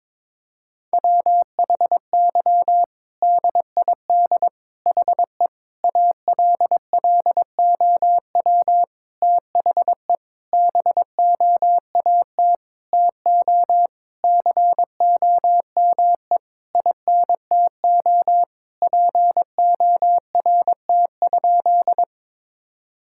Morse code, heard as a keyed tone, about 22 wpm, 700 Hz. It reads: WHY DID HE ALLOW THE BOAT TO COME INTO PORT?
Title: Sentences from Top 1000 Words - Review Part 5 - 22wpm